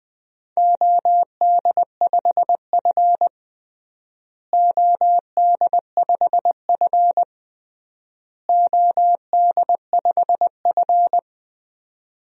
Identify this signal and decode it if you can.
Morse code, heard as a keyed tone, about 20 wpm, 700 Hz. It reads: OD5F OD5F OD5F
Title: DXCC-100 Call Signs - 4 Characters - Repeated 3x - 20wpm